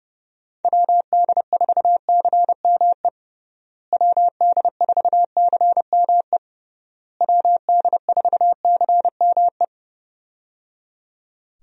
Morse code, heard as a keyed tone, about 30 wpm, 700 Hz. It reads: WD4CME WD4CME WD4CME
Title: USA Call Signs - 6 Characters - Repeated 3x - 30wpm